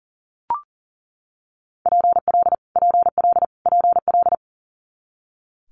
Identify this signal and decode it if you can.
Morse code, heard as a keyed tone, about 38 wpm, 700 Hz. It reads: PL PL PL
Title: Multi-letter Phonemes Repeated 3x 40wpm